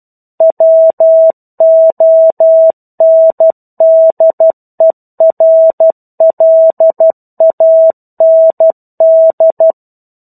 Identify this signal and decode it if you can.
Morse code, heard as a keyed tone, about 12 wpm, 650 Hz. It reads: WONDERLAND
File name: 01-Chapter-01 12 WPM